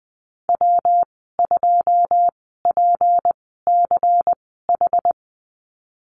Morse code, heard as a keyed tone, about 19 wpm, spaced as for 14 wpm, 700 Hz. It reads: W2PCH
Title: 186 Lesson 30g - Letter b - Call Signs 20wpm